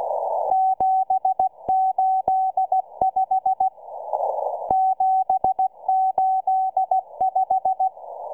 Morse code, heard as a keyed tone, about 16 wpm, 760 Hz. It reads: 785 785